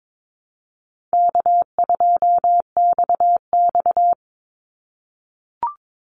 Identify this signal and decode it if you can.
Morse code, heard as a keyed tone, about 22 wpm, 700 Hz. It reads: K2XX
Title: Easy Call Signs - 22wpm